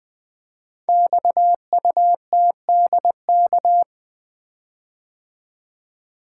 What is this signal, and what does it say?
Morse code, heard as a keyed tone, about 20 wpm, 700 Hz. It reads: XUTDK
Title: Code Groups - 5 Letters - 20wpm